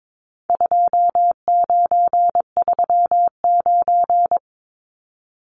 Morse code, heard as a keyed tone, about 22 wpm, 700 Hz. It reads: 2939
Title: Numbers - 4-Digit 22wpm